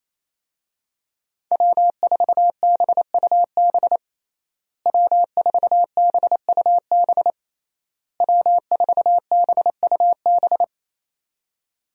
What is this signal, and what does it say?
Morse code, heard as a keyed tone, about 28 wpm, 700 Hz. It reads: W4BUB W4BUB W4BUB
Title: USA Call Signs - 5 Characters - Repeated 3x - 28wpm